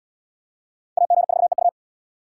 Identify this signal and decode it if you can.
Morse code, heard as a keyed tone, about 67 wpm, 700 Hz. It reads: AG4EU